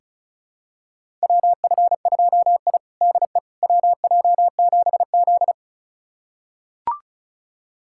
Morse code, heard as a keyed tone, about 36 wpm, 700 Hz. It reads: WF2I DE WJ7Z
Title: USA Callsign DE Callsign - 4 Characters - 35wpm